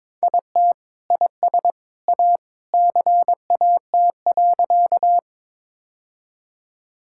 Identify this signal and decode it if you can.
Morse code, heard as a keyed tone, about 22 wpm, 700 Hz. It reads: IT IS A CAT.
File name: practice / 22wpm / Sentences with 2-Word Phrases - 22wpm